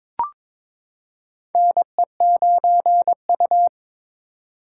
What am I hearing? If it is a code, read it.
Morse code, heard as a keyed tone, about 22 wpm, 700 Hz. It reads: NE9U
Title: K1USN SST QSO Practice - 22wpm